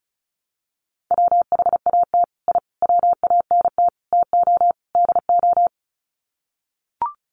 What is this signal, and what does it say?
Morse code, heard as a keyed tone, about 34 wpm, 700 Hz. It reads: WHAT I WANT TO DO
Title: Sets of 5 Words Encore 35wpm